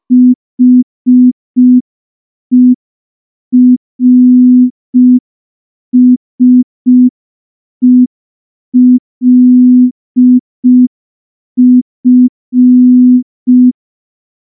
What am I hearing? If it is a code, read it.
Morse code, heard as a keyed tone, about 5 wpm, 250 Hz. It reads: HERSELF
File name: morse / morse-training-05WPM